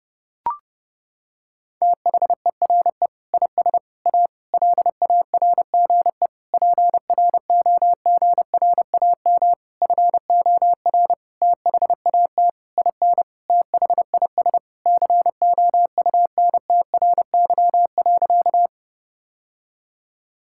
Morse code, heard as a keyed tone, about 30 wpm, 700 Hz. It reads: THERE IS A LARGE PROGRAM FOR THAT IN THIS COUNTRY.
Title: Sentences from Top 1000 Words - Review Part 5 - 30wpm